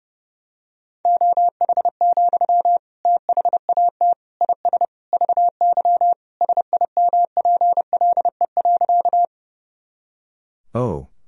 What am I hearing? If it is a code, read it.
Morse code, heard as a keyed tone, about 30 wpm, 700 Hz. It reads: OH, THAT IS VY SIMPLE.
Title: Sentences from Top 500 Words and Abbreviations 30wpm